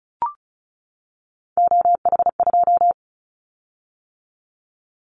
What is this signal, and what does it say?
Morse code, heard as a keyed tone, about 34 wpm, 700 Hz. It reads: OH2
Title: Callsign Prefixes for Top 25 DXCC - 35wpm